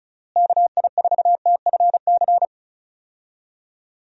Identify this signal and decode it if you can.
Morse code, heard as a keyed tone, about 34 wpm, 700 Hz. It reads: KI4TFC
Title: USA Call Signs - 6 Characters - Repeated 3x - 35wpm